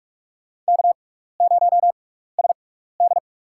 Morse code, heard as a keyed tone, about 46 wpm, spaced as for 15 wpm, 700 Hz. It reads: K0SD
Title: USA Call Signs - 4 Characters - 6x Character Spacing - 45wpm